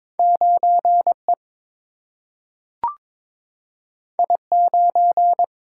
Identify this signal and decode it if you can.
Morse code, heard as a keyed tone, about 22 wpm, 700 Hz. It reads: 9E I9
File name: practice / 22wpm / Two Letter-Number Combinations - 22wpm